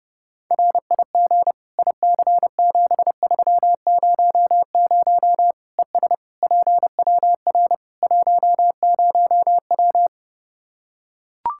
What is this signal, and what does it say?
Morse code, heard as a keyed tone, about 30 wpm, 700 Hz. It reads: RIG IC7300 ES PWR 10W
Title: K1USN SST QSO Practice 30wpm